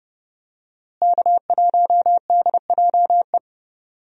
Morse code, heard as a keyed tone, about 32 wpm, 700 Hz. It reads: K1DJE
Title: USA Call Signs - 5 Characters - Repeated 3x - 30wpm